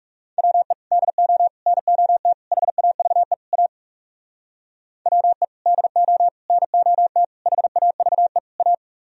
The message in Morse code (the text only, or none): WE DO NOT HAVE A WE DO NOT HAVE A